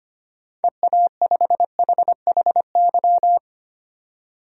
Morse code, heard as a keyed tone, about 25 wpm, 700 Hz. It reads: EA5HHY
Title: DXCC-100 Call Signs - 6 Characters - 25wpm